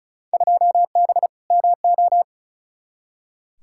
Morse code, heard as a keyed tone, about 36 wpm, 700 Hz. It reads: 2B MO
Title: Field Day Exchanges - 35wpm